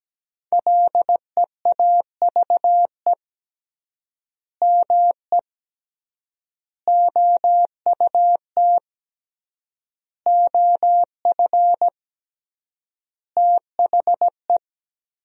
Morse code, heard as a keyed tone, about 17 wpm, 700 Hz. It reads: LEAVE ME OUT OF THE
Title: 3x Word-Spacing - Sentences from Top 400 Words - 17wpm